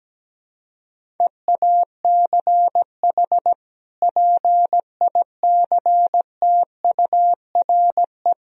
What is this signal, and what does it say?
Morse code, heard as a keyed tone, about 17 wpm, 700 Hz. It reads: EACH PICTURE
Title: Sentences from Top 1000 Words - Review Part 1 - 17wpm